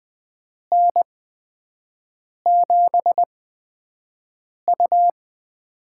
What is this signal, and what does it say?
Morse code, heard as a keyed tone, about 20 wpm, spaced as for 5 wpm, 700 Hz. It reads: N7U
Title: USA Call Signs - 4 Characters - 8x Character Spacing - 20wpm